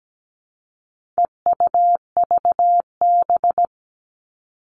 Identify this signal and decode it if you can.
Morse code, heard as a keyed tone, about 17 wpm, 700 Hz. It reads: EUVB
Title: All Cut Numbers - 4-Digit 17wpm